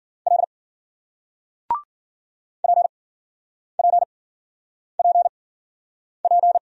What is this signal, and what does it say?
Morse code, heard as a keyed tone, about 56 wpm, spaced as for 9 wpm, 700 Hz. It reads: L PPPP